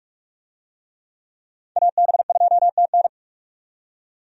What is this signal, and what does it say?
Morse code, heard as a keyed tone, about 47 wpm, 700 Hz. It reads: AB2TN
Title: USA Call Signs - 5 Characters - 45wpm